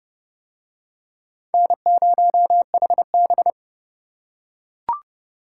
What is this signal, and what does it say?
Morse code, heard as a keyed tone, about 30 wpm, 700 Hz. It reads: N0HB